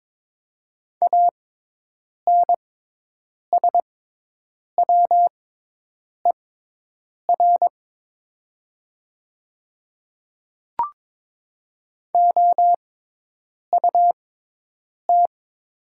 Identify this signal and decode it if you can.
Morse code, heard as a keyed tone, about 23 wpm, spaced as for 8 wpm, 700 Hz. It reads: ANSWER OUT
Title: 6x Character Spacing - Top 300 Words - 22wpm